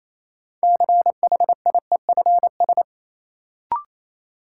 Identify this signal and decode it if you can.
Morse code, heard as a keyed tone, about 28 wpm, 700 Hz. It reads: CHIEFS
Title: Sets of 5 Words Encore - New Word Drill - 28wpm